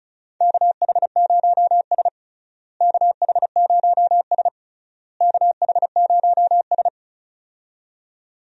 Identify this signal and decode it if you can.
Morse code, heard as a keyed tone, about 34 wpm, 700 Hz. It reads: KH0S KH0S KH0S
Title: USA Call Signs - 4 Characters - Repeated 3x - 35wpm